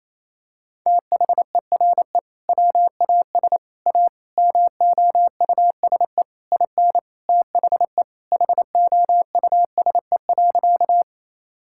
Morse code, heard as a keyed tone, about 28 wpm, 700 Hz. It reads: THERE WAS A MOUSE IN THE HOUSE.
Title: Sentences from Top 1000 Words - Review Part 5 - 28wpm